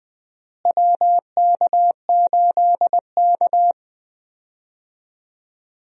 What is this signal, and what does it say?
Morse code, heard as a keyed tone, about 20 wpm, 700 Hz. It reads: WK8K
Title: USA Call Signs - 4 Characters - Repeated 2x - 20wpm